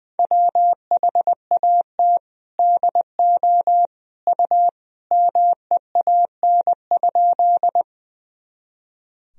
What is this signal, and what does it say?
Morse code, heard as a keyed tone, about 20 wpm, 700 Hz. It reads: WHAT DO U MEAN?
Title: Sentences from Top 200 Words and Abbreviations 20wpm